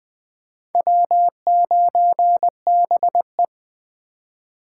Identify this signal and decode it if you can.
Morse code, heard as a keyed tone, about 24 wpm, 700 Hz. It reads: W9BE
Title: Speed Racing - USA 1x2 Call Signs 20wpm